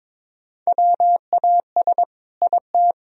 Morse code, heard as a keyed tone, about 21 wpm, 700 Hz. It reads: WAS IT
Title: Sentences from Top 1000 Words - Review Part 4 - 22wpm